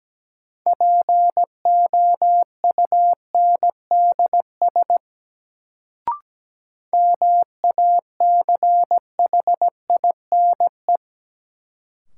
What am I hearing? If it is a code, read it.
Morse code, heard as a keyed tone, about 17 wpm, 700 Hz. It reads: POUNDS MACHINE
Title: Sentences from Top 1000 Words - Review Part 2 - New Word Drill - 17wpm